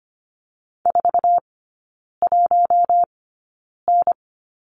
Morse code, heard as a keyed tone, about 25 wpm, spaced as for 9 wpm, 700 Hz. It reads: 41N
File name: course / 25:10wpm / 093 Lesson 16e - Letter c - Three Characters 25wpm